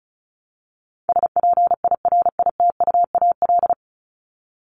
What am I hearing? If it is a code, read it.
Morse code, heard as a keyed tone, about 34 wpm, 700 Hz. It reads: SPIRITUAL
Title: Top 1501-1750 Words 35wpm